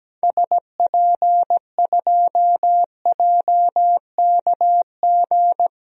Morse code, heard as a keyed tone, about 17 wpm, 700 Hz. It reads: SP2JKG
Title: DXCC-100 Call Signs - 6 Characters - 17wpm